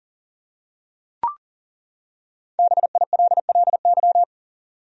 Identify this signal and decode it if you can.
Morse code, heard as a keyed tone, about 40 wpm, 700 Hz. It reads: BILLY
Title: USA Common Names - Max 5 Letters - 40wpm